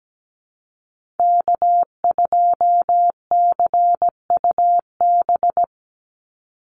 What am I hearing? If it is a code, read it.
Morse code, heard as a keyed tone, about 17 wpm, 700 Hz. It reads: K2CUB
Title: Call Signs Repeated 3x - 17wpm